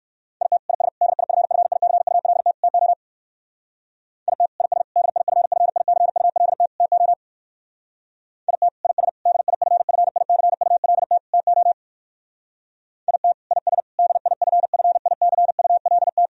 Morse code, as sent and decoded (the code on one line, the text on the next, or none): .. - / .. ... / -.. .. ..-. ..-. .. -.-. ..- .-.. - / - --- / .. - / .. ... / -.. .. ..-. ..-. .. -.-. ..- .-.. - / - --- / .. - / .. ... / -.. .. ..-. ..-. .. -.-. ..- .-.. - / - --- / .. - / .. ... / -.. .. ..-. ..-. .. -.-. ..- .-.. -
IT IS DIFFICULT TO IT IS DIFFICULT TO IT IS DIFFICULT TO IT IS DIFFICULT